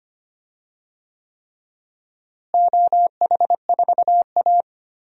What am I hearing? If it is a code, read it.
Morse code, heard as a keyed tone, about 25 wpm, 700 Hz. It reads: OH4A
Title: DXCC-100 Call Signs - 4 Characters - Repeated 3x - 25wpm